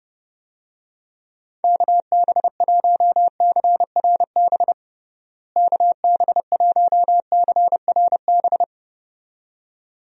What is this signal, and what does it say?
Morse code, heard as a keyed tone, about 30 wpm, 700 Hz. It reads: KB1CRB KB1CRB
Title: USA Call Signs - 6 Characters - Repeated 2x - 30wpm